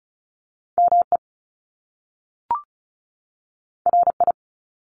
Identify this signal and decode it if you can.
Morse code, heard as a keyed tone, about 36 wpm, 700 Hz. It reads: ME RI
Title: Top Two Letter Combinations in Words 35wpm